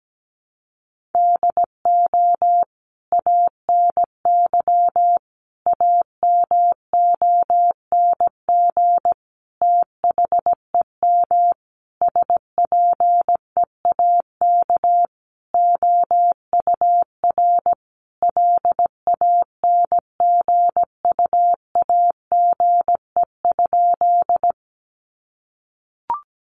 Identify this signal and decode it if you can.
Morse code, heard as a keyed tone, about 17 wpm, 700 Hz. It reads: DO ANY AMONG THEM SPEAK OUR LANGUAGE?
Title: Sentences from Top 1100 Words 17wpm